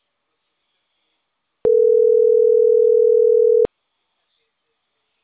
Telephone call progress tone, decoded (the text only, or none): ringback tone